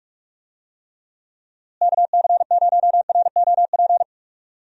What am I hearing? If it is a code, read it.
Morse code, heard as a keyed tone, about 46 wpm, 700 Hz. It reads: KC0ROP
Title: USA Call Signs - 6 Characters - Repeated 2x - 45wpm